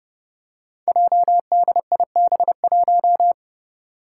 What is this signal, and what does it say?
Morse code, heard as a keyed tone, about 29 wpm, 700 Hz. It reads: JDIB1
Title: Five Challenging Letter-Number Combinations 30wpm